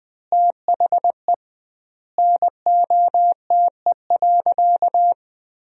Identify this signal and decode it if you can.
Morse code, heard as a keyed tone, about 20 wpm, 700 Hz. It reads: THE NOTE.
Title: 2x Word-Spacing - Sentences from Top 500 Words - 20wpm